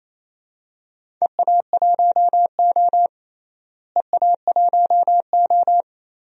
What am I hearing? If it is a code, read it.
Morse code, heard as a keyed tone, about 28 wpm, 700 Hz. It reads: EA1O EA1O
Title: DXCC-100 Call Signs - 4 Characters - Repeated 2x - 28wpm